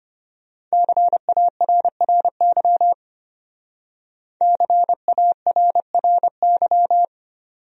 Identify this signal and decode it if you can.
Morse code, heard as a keyed tone, about 32 wpm, 700 Hz. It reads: CARRY CARRY